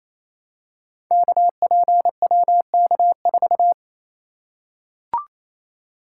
Morse code, heard as a keyed tone, about 28 wpm, 700 Hz. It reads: KPWK4
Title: Code Groups - 5 Letter:Numbers - 28wpm